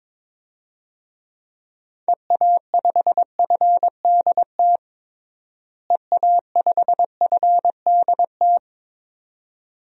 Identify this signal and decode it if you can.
Morse code, heard as a keyed tone, about 22 wpm, 700 Hz. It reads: EA5FDT EA5FDT
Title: DXCC-100 Call Signs - 6 Characters - Repeated 2x - 22wpm